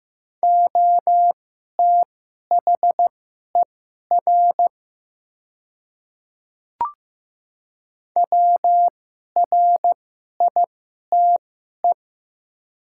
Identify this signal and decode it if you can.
Morse code, heard as a keyed tone, about 15 wpm, spaced as for 11 wpm, 700 Hz. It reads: OTHER WRITE
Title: 2x Character Spacing - Top 100 Words - 15wpm